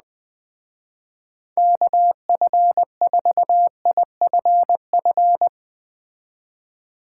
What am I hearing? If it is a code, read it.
Morse code, heard as a keyed tone, about 20 wpm, 700 Hz. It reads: KF4IFF
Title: Speed Racing - USA Call Signs 17wpm